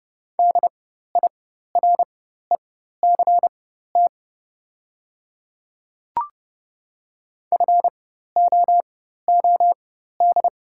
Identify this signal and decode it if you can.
Morse code, heard as a keyed tone, about 30 wpm, spaced as for 14 wpm, 700 Hz. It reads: DIRECT FOOD
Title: 4x Character Spacing - Top 500 Words - 30wpm